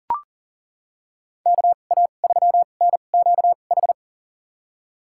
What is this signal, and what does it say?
Morse code, heard as a keyed tone, about 40 wpm, spaced as for 29 wpm, 700 Hz. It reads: KA3NQH
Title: USA Call Signs - 2x Character Spacing - 40wpm